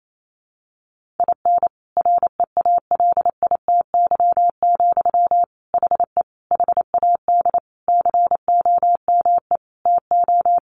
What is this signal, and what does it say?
Morse code, heard as a keyed tone, about 28 wpm, 700 Hz. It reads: IN REALITY, HE HAD COME TO
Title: Sentences from Top 1500 Words 28wpm